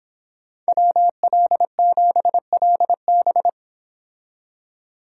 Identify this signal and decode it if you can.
Morse code, heard as a keyed tone, about 26 wpm, 700 Hz. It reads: WL7LB